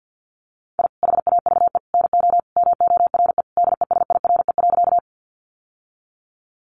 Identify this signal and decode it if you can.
Morse code, heard as a keyed tone, about 50 wpm, 700 Hz. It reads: I HAVE NO MORE DESIRE.
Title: Sentences from Top 1100 Words 50wpm